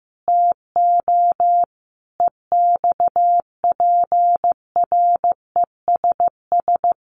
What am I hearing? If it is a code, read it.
Morse code, heard as a keyed tone, about 15 wpm, 700 Hz. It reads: TO EXPRESS
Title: Sentences from Top 1100 Words 15wpm